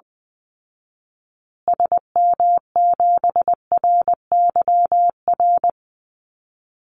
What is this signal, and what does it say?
Morse code, heard as a keyed tone, about 20 wpm, 700 Hz. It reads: SM7RYR
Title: Call Signs Repeated 2x - 20wpm